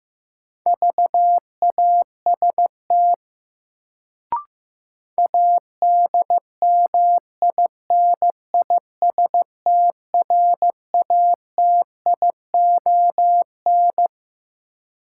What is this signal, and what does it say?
Morse code, heard as a keyed tone, about 15 wpm, 700 Hz. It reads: VAST ADMINISTRATION
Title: Sentences from Top 1300 Words - New Word Drill Alt - 15wpm